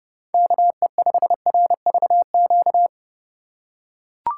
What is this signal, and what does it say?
Morse code, heard as a keyed tone, about 32 wpm, 700 Hz. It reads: KE5RVQ
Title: USA Call Signs - 6 Characters - 3x Character Spacing - 30wpm